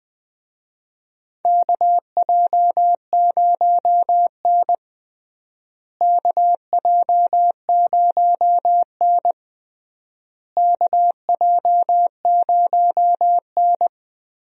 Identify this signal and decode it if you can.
Morse code, heard as a keyed tone, about 20 wpm, 700 Hz. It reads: KJ0N KJ0N KJ0N
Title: USA Call Signs - 4 Characters - Repeated 3x - 20wpm